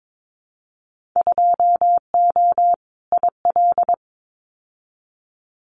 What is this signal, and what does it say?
Morse code, heard as a keyed tone, about 22 wpm, 700 Hz. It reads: 2O IL
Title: Winter Field Day Exchanges 22wpm